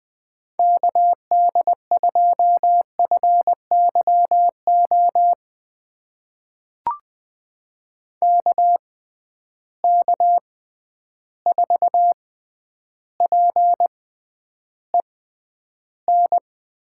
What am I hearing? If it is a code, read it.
Morse code, heard as a keyed tone, about 21 wpm, spaced as for 10 wpm, 700 Hz. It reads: KD2FYO KK4PEN
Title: USA Call Signs - 6 Characters - 6x Character Spacing - 20wpm